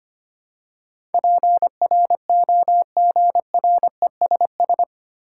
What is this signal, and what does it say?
Morse code, heard as a keyed tone, about 25 wpm, 700 Hz. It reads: PROGRESS
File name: Eight Letter Words - 25wpm